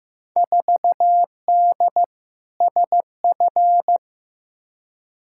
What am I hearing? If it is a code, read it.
Morse code, heard as a keyed tone, about 16 wpm, 700 Hz. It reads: 4D SF
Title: Field Day Exchanges Encore - 15wpm